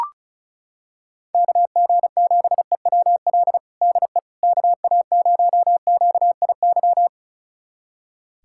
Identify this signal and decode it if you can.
Morse code, heard as a keyed tone, about 36 wpm, 700 Hz. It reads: KG7EWL DE KA0QIY
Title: USA Callsign DE Callsign - 6 Characters - 35wpm